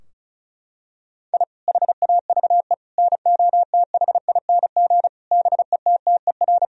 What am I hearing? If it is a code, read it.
Morse code, heard as a keyed tone, about 34 wpm, 700 Hz. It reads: I HAVE NOTHING BETTER
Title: Sentences from Top 1000 Words - Review Part 4 - 35wpm